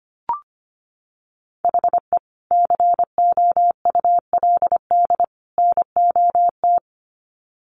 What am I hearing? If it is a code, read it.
Morse code, heard as a keyed tone, about 25 wpm, 700 Hz. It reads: HE COULD NOT
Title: Sets of 3 Words 25wpm